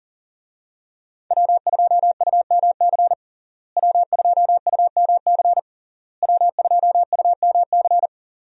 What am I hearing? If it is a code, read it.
Morse code, heard as a keyed tone, about 40 wpm, 700 Hz. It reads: W2UMC W2UMC W2UMC
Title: USA Call Signs - 5 Characters - Repeated 3x - 40wpm